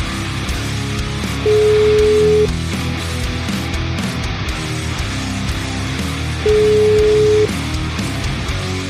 Ringback tone (Europe, 425 Hz), with noise.